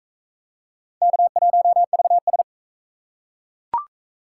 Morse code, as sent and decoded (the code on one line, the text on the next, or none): -.- .---- ...- ...
K1VS